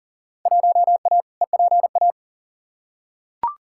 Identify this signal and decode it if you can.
Morse code, heard as a keyed tone, about 40 wpm, 700 Hz. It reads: 1A EPA
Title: Field Day Exchanges - 40wpm